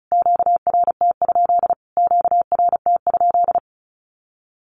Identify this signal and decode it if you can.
Morse code, heard as a keyed tone, about 34 wpm, 700 Hz. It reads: QRT? QRT?
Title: Q-Codes Repeated 3x 35wpm